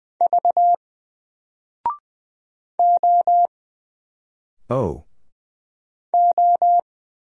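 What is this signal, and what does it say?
Morse code, heard as a keyed tone, about 20 wpm, 700 Hz. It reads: V O O
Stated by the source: Single letter and number Morse drill at 20 wpm, from N4LCD.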